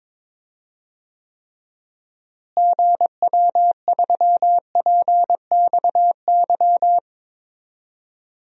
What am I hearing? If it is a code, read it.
Morse code, heard as a keyed tone, about 22 wpm, 700 Hz. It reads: GW3PXY
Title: DXCC-100 Call Signs - 6 Characters - Repeated 3x - 22wpm